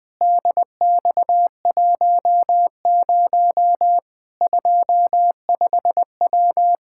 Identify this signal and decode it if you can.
Morse code, heard as a keyed tone, about 20 wpm, 700 Hz. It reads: DX10 25W
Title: Practice QSO - Exchange 2 - Info - 20wpm